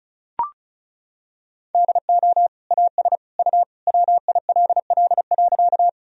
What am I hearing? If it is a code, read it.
Morse code, heard as a keyed tone, about 36 wpm, 700 Hz. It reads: DO AS U WILL.
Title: Sentences from Top 100 Words and Abbreviations 35wpm